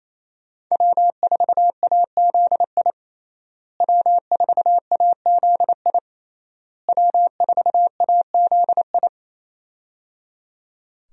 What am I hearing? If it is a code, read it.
Morse code, heard as a keyed tone, about 28 wpm, 700 Hz. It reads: W4AZI W4AZI W4AZI
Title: USA Call Signs - 5 Characters - Repeated 3x - 28wpm